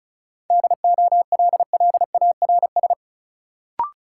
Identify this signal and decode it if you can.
Morse code, heard as a keyed tone, about 34 wpm, 700 Hz. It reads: DOLLARS
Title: Sets of 5 Words - New Word Drill - 35wpm